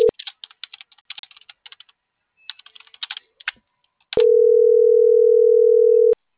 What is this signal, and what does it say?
Ringback tone (North America), 440 + 480 Hz.